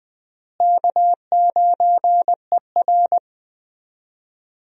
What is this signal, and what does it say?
Morse code, heard as a keyed tone, about 21 wpm, 700 Hz. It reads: K9ER
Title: USA Call Signs - 4 Characters - Repeated 3x - 20wpm